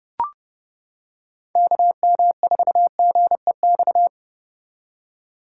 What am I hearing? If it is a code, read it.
Morse code, heard as a keyed tone, about 31 wpm, 700 Hz. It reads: KM4GEX